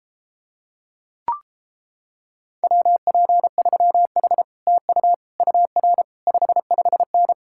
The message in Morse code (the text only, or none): WP3H TU UR 55N